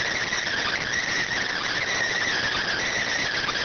Emergency Alert System (SAME) tones, heard as harsh buzzing bursts. It is heard from 0.4 to 3.0 s.